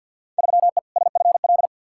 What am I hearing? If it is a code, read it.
Morse code, heard as a keyed tone, about 51 wpm, 700 Hz. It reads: 3E SFL
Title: Field Day Exchanges Encore - 50wpm